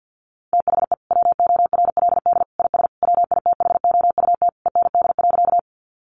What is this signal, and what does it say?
Morse code, heard as a keyed tone, about 50 wpm, 700 Hz. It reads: THE WORLD IS WITHOUT END.
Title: Sentences from Top 600 Words 50wpm